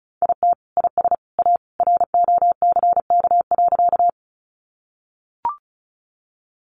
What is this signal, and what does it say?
Morse code, heard as a keyed tone, about 34 wpm, 700 Hz. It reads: IT IS A ROCK.
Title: Sentences from Top 400 Words 35wpm